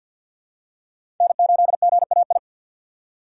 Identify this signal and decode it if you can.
Morse code, heard as a keyed tone, about 48 wpm, 690 Hz. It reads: N8GAI
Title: USA Call Signs - 5 Characters - 4x Character Spacing - 50wpm